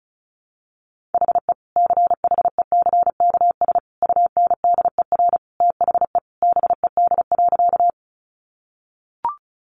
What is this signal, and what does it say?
Morse code, heard as a keyed tone, about 34 wpm, 700 Hz. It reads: HE CHECKS UNDER THE BED.
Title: Easy Phrases 35wpm